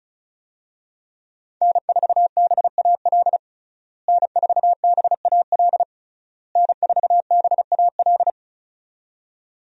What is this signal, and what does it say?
Morse code, heard as a keyed tone, about 36 wpm, 700 Hz. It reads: N4BAL N4BAL N4BAL
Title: USA Call Signs - 5 Characters - Repeated 3x - 35wpm